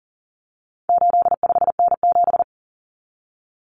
Morse code, heard as a keyed tone, about 40 wpm, 700 Hz. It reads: 85N7
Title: Common Cut Numbers - 4-Digit 40wpm